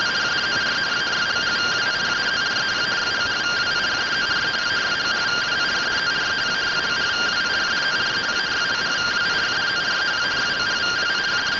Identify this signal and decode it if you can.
Two-tone teleprinter signal, about 170 Hz shift.